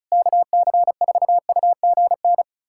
Morse code, heard as a keyed tone, about 34 wpm, 700 Hz. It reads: KC4UGN
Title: USA Callsign DE Callsign 35wpm